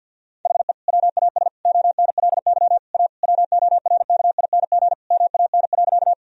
Morse code, heard as a keyed tone, about 50 wpm, 700 Hz. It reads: HE WAS ONLY A WORKING MAN.
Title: Sentences from Top 1000 Words - Review Part 4 - 50wpm